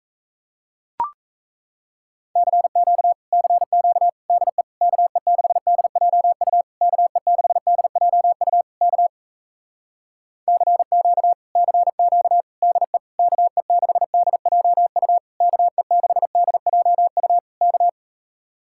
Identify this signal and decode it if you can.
Morse code, heard as a keyed tone, about 38 wpm, 700 Hz. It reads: CQ CQ DE KE6DJU KE6DJU K CQ CQ DE KE6DJU KE6DJU K